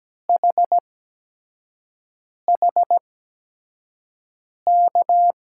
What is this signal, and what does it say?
Morse code, heard as a keyed tone, about 17 wpm, spaced as for 5 wpm, 700 Hz. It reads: HHK
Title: USA Call Signs - 6 Characters - 8x Character Spacing - 17wpm